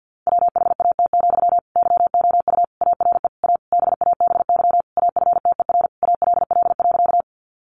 Morse code, heard as a keyed tone, about 49 wpm, 700 Hz. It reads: WHAT, YOU ARE A BABY AFTER ALL.
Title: Sentences from Top 1300 Words 50wpm